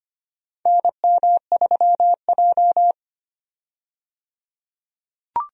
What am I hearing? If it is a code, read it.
Morse code, heard as a keyed tone, about 25 wpm, 700 Hz. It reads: NM3J
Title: USA Call Signs - 4 Characters - 25wpm